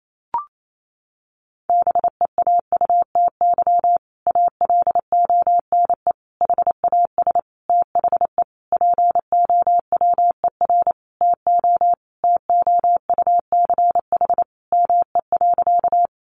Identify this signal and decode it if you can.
Morse code, heard as a keyed tone, about 28 wpm, 700 Hz. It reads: BEAUTY ALONE HAS THE POWER TO TOUCH ME.
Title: Sentences from Top 1200 Words 28wpm